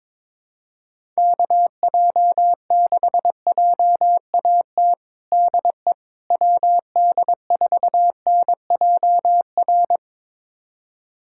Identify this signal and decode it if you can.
Morse code, heard as a keyed tone, about 22 wpm, 700 Hz. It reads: KJ6JAT DE WD4NJR
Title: USA Callsign DE Callsign - 6 Characters - 22wpm